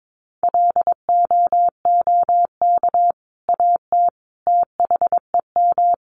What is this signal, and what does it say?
Morse code, heard as a keyed tone, about 22 wpm, 700 Hz. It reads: LOOK AT THEM
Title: Sentences from Top 1400 Words 22wpm